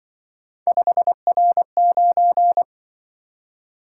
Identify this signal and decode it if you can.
Morse code, heard as a keyed tone, about 25 wpm, 700 Hz. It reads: 5R9
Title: Speed-Racing - Three Letter-Number Combinations - 20wpm